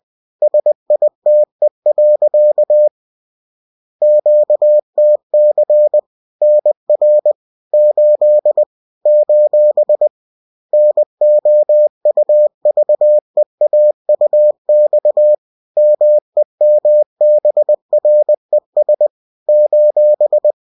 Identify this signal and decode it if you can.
Morse code, heard as a keyed tone, about 20 wpm, 600 Hz. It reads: SITE. QTC NR 8 : NOUVEAUX MEMBRES :